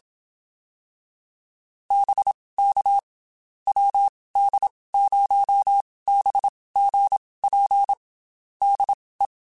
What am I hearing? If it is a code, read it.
Morse code, heard as a keyed tone, about 27 wpm, spaced as for 19 wpm, 800 Hz. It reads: BK WD0BGP DE
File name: QSOs / QSOs-mp3s / WD0BGP 1@25